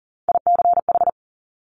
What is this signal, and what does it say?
Morse code, heard as a keyed tone, about 40 wpm, 690 Hz. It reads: ICH
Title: Top Three Letter Combinations in Words 40wpm